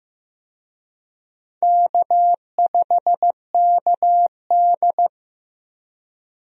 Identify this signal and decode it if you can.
Morse code, heard as a keyed tone, about 15 wpm, 700 Hz. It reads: K5KD
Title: USA Call Signs - 4 Characters - Repeated 2x - 15wpm